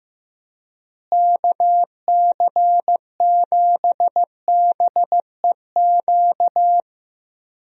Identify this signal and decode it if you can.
Morse code, heard as a keyed tone, about 15 wpm, 700 Hz. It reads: KC7BEQ
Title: USA Call Signs - 6 Characters - Repeated 2x - 15wpm